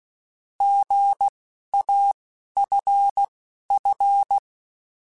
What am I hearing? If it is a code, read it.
Morse code, heard as a keyed tone, about 16 wpm, 800 Hz. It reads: G A F F
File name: Misc / Misc-mp3s / Misc @10